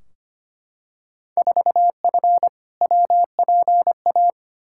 Morse code, heard as a keyed tone, about 25 wpm, 700 Hz. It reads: 4F WPA
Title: Field Day Exchanges - 25wpm